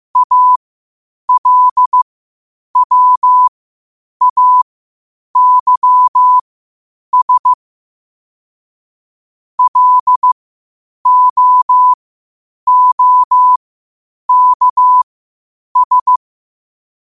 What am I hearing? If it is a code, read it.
Morse code, heard as a keyed tone, about 15 wpm, spaced as for 8 wpm, 1000 Hz. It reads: ALWAYS LOOKS
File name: qotdfiles / 8wpm